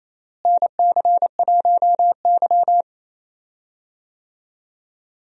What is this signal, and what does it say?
Morse code, heard as a keyed tone, about 28 wpm, 700 Hz. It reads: NC1Y
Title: USA Call Signs - 4 Characters - Repeated 2x - 28wpm